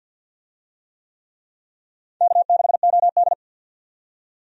Morse code, heard as a keyed tone, about 49 wpm, 700 Hz. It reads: K6OD
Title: USA Call Signs - 4 Characters - 50wpm